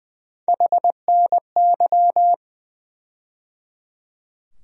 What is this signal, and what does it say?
Morse code, heard as a keyed tone, about 20 wpm, 700 Hz. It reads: HNY
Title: USA Call Sign Postfixes - 3 Letters 20wpm